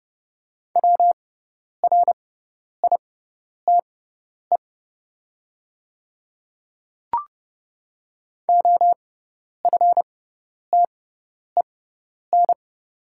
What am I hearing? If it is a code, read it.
Morse code, heard as a keyed tone, about 30 wpm, spaced as for 10 wpm, 700 Hz. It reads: WRITE OFTEN
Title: 6x Character Spacing - Top 300 Words - 30wpm